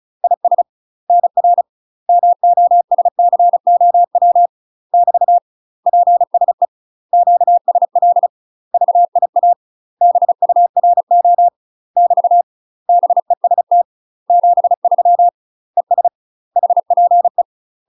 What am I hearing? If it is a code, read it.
Morse code, heard as a keyed tone, about 34 wpm, 700 Hz. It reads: IS NR MOSCOW <BT> PSE QSL VIA BURO <BT> BEST 73 ES HPE
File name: UK MASTER QSOs TRAINING / 35WPM / UK MASTER QSOs 0039